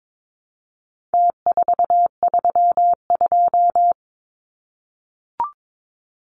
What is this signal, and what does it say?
Morse code, heard as a keyed tone, about 22 wpm, 700 Hz. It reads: T432
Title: Numbers - Contest Style - 4-Digit 22wpm